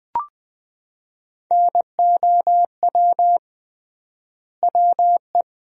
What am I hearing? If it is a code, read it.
Morse code, heard as a keyed tone, about 20 wpm, 700 Hz. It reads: NOW WE
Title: 3x Word-Spacing - Sentences from Top 500 Words - 20wpm